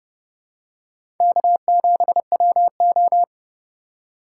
Morse code, heard as a keyed tone, about 30 wpm, 700 Hz. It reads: K7WO
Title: USA Call Signs - 4 Characters - 2x Character Spacing - 30wpm